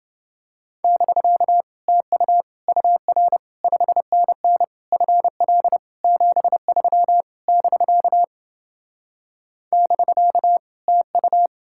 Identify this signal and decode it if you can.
Morse code, heard as a keyed tone, about 29 wpm, 700 Hz. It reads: <BK> TU UR 5NN FL 73 <BK> <BK> TU